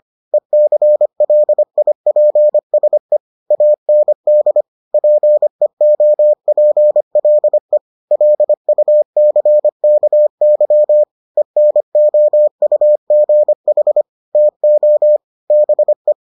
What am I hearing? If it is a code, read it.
Morse code, heard as a keyed tone, about 25 wpm, 600 Hz. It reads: ECLIPSE AND PEOPLE LUCKY ENOUGH TO BE